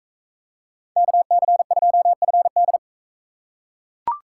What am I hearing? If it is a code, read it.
Morse code, heard as a keyed tone, about 41 wpm, 700 Hz. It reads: KC2FD